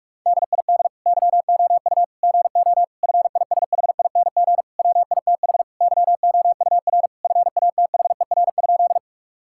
Morse code, heard as a keyed tone, about 41 wpm, 700 Hz. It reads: DID YOU GO FISHING WITH YOUR FATHER?
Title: Sentences from Top 1000 Words - Review Part 5 - 45wpm